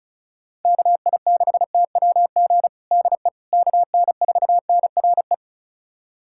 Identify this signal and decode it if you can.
Morse code, heard as a keyed tone, about 34 wpm, 690 Hz. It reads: KI6TWG DE KN4NRE
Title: USA Callsign DE Callsign - 6 Characters - 35wpm